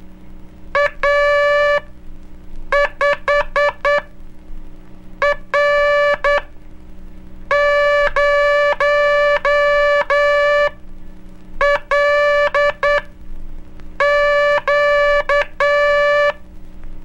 Morse code, heard as a keyed tone, about 9 wpm, spaced as for 5 wpm, 580 Hz. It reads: A5R0LQ
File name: cursos / Curso CW mp3 / CW Aula - 04